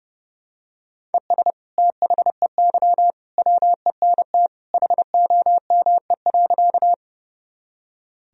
Morse code, heard as a keyed tone, about 32 wpm, 700 Hz. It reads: ES THEY WENT HOME.
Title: Sentences from Top 200 Words and Abbreviations 30wpm